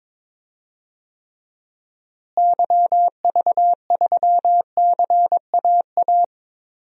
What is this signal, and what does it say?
Morse code, heard as a keyed tone, about 21 wpm, 700 Hz. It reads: YV3CAA
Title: DXCC-100 Call Signs - 6 Characters - Repeated 2x - 22wpm